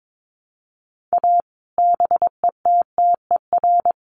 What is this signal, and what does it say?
Morse code, heard as a keyed tone, about 22 wpm, 700 Hz. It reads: A BETTER
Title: Sets of 5 Words Encore2 22wpm